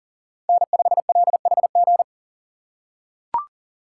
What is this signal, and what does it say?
Morse code, heard as a keyed tone, about 40 wpm, 700 Hz. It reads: N5LHG
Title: USA Call Signs - 5 Characters - 3x Character Spacing - 40wpm